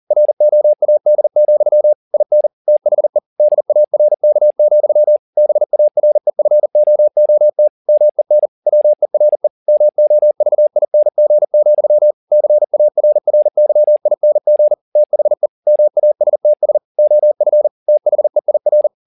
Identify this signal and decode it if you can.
Morse code, heard as a keyed tone, about 40 wpm, 600 Hz. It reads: ROAD, IN THE DARK, BAREFOOT MEN WERE MOVING, CARRYING THE MASTS OF THEIR